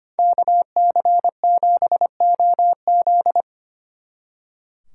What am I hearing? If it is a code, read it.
Morse code, heard as a keyed tone, about 31 wpm, 700 Hz. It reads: KC7OZ
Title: Speed Racing - USA 2x2 Call Signs 25wpm